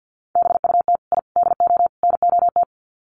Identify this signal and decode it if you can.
Morse code, heard as a keyed tone, about 48 wpm, 700 Hz. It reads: BUT I DO NOT
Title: Sets of 4 Words 50wpm